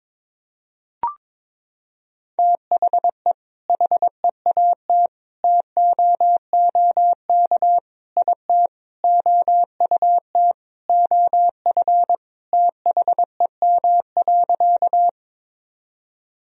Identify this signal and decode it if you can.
Morse code, heard as a keyed tone, about 22 wpm, 700 Hz. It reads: THE HEAT TOOK IT OUT OF THEM.
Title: Sentences from Top 1000 Words - Review Part 3 - 22wpm